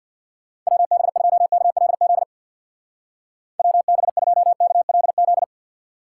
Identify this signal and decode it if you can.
Morse code, heard as a keyed tone, about 60 wpm, 700 Hz. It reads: WB2KLZ WB2KLZ
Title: Speed Racing - USA 2x3 Call Signs 50wpm